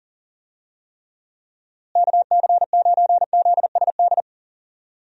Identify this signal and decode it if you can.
Morse code, heard as a keyed tone, about 40 wpm, 700 Hz. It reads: KC9ZSD
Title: USA Call Signs - 6 Characters - Repeated 3x - 40wpm